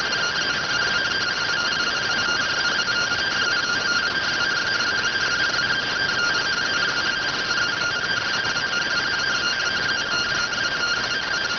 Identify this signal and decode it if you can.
Two-tone teleprinter signal, about 170 Hz shift.